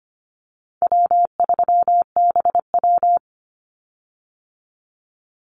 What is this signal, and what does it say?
Morse code, heard as a keyed tone, about 25 wpm, 700 Hz. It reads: W3BW
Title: Easy Call Signs - 25wpm